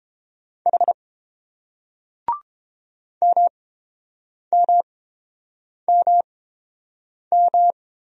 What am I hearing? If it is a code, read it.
Morse code, heard as a keyed tone, about 31 wpm, spaced as for 7 wpm, 690 Hz. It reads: H MMMM